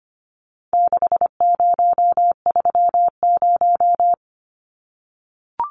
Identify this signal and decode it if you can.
Morse code, heard as a keyed tone, about 25 wpm, 700 Hz. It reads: 6030
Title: Numbers - 4-Digit 25wpm